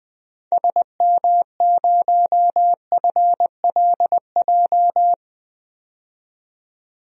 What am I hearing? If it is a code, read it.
Morse code, heard as a keyed tone, about 20 wpm, 700 Hz. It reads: SM0FLJ
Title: DXCC-100 Call Signs - 6 Characters - 20wpm